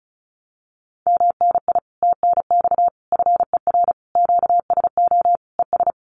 Morse code, heard as a keyed tone, about 34 wpm, 700 Hz. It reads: MNI TNX FER QSO ES
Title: Common QSO Phrases Encore 35wpm